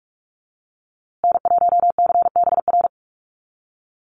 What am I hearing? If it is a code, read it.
Morse code, heard as a keyed tone, about 44 wpm, 700 Hz. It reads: N1CBR
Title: USA Call Signs Repeated 3x 45wpm